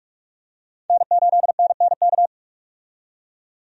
Morse code, heard as a keyed tone, about 42 wpm, 700 Hz. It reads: N8NNK
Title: USA Call Signs - 5 Characters - 45wpm